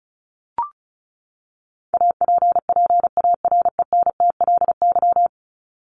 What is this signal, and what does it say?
Morse code, heard as a keyed tone, about 34 wpm, 700 Hz. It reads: APPARENTLY
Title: LY Suffix - Top 100 Words with Base Words 35wpm